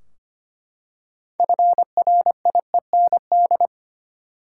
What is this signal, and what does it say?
Morse code, heard as a keyed tone, about 25 wpm, 700 Hz. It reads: FRIEND
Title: Six Letter Words - 25wpm